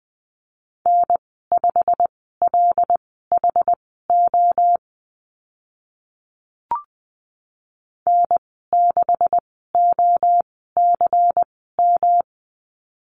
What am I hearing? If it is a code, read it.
Morse code, heard as a keyed tone, about 20 wpm, spaced as for 14 wpm, 700 Hz. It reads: N5LHO N6OCM
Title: 154 Lesson 25g - Letter p - Call Signs 20wpm